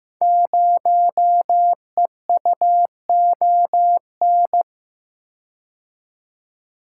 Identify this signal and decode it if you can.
Morse code, heard as a keyed tone, about 15 wpm, 700 Hz. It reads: 0EUON
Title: Code Groups - 5 Letter:Numbers - 15wpm